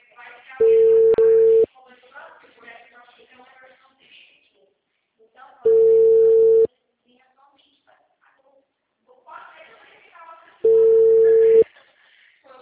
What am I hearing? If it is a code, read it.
Ringback tone (China, 450 Hz), with speech.